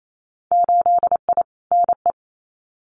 Morse code, heard as a keyed tone, about 28 wpm, 700 Hz. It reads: 8I NE
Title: Winter Field Day Exchanges Encore 28wpm